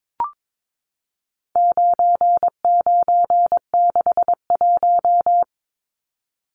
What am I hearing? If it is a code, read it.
Morse code, heard as a keyed tone, about 22 wpm, 700 Hz. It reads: 9961
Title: Numbers - 4-Digit 22wpm